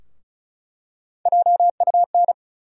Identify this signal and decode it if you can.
Morse code, heard as a keyed tone, about 36 wpm, 700 Hz. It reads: JUN
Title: USA Call Sign Postfixes - 3 Letters 35wpm